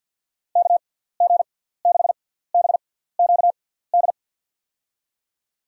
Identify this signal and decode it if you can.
Morse code, heard as a keyed tone, about 52 wpm, spaced as for 17 wpm, 700 Hz. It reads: KG6BQD
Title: USA Call Signs - 6 Characters - 6x Character Spacing - 50wpm